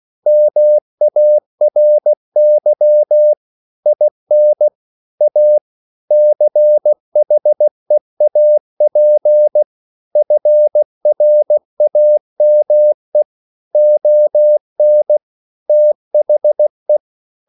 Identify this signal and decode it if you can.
Morse code, heard as a keyed tone, about 16 wpm, 600 Hz. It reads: MARY IN A CHEAP FRAME ON THE